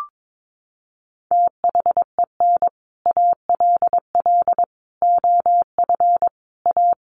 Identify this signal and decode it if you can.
Morse code, heard as a keyed tone, about 22 wpm, 700 Hz. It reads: THEN ALL OF A
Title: Sets of 5 Words Encore 22wpm